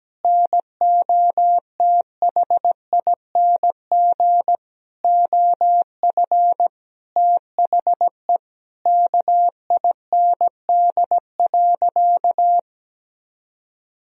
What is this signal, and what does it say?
Morse code, heard as a keyed tone, about 17 wpm, 700 Hz. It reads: NOTHING OF THE KIND.
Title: Sentences from Top 500 Words and Abbreviations 17wpm